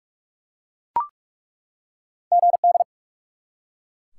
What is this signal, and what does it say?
Morse code, heard as a keyed tone, about 43 wpm, 700 Hz. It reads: GD
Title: Top 180 QSO Elements 45wpm